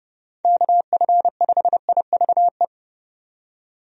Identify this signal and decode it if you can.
Morse code, heard as a keyed tone, about 32 wpm, 700 Hz. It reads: KF5IVE
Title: USA Call Signs - 6 Characters - 30wpm